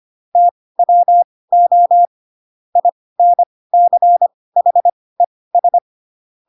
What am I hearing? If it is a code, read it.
Morse code, heard as a keyed tone, about 25 wpm, spaced as for 18 wpm, 700 Hz. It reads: TWO INCHES